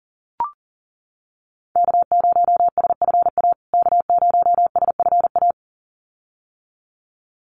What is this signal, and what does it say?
Morse code, heard as a keyed tone, about 40 wpm, 700 Hz. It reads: K0SFA K0SFA
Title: USA Call Signs Repeated 2x 40wpm